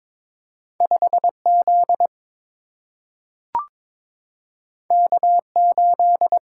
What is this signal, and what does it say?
Morse code, heard as a keyed tone, about 22 wpm, 700 Hz. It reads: 5Z K8
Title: Single-Letter Single-Number 22wpm